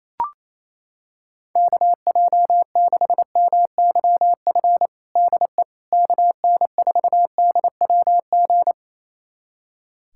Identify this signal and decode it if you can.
Morse code, heard as a keyed tone, about 28 wpm, 700 Hz. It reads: KJ6MYF DE KN4DWG
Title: USA Callsign DE Callsign 28wpm